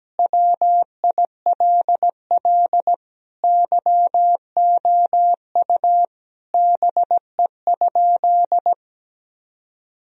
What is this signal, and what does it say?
Morse code, heard as a keyed tone, about 17 wpm, 700 Hz. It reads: WILL YOU BE?
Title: Sentences from Top 1000 Words - Review Part 4 - 17wpm